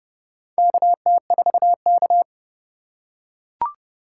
Morse code, heard as a keyed tone, about 30 wpm, 700 Hz. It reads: KT4K
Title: USA Call Signs - 4 Characters - 4x Character Spacing - 30wpm